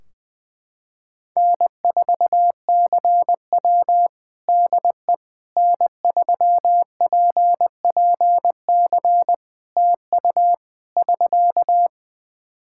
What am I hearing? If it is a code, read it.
Morse code, heard as a keyed tone, about 20 wpm, 700 Hz. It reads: N4CW DE N3PPC TU <SK>
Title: Practice QSO - Exchange 4 - Ending - 20wpm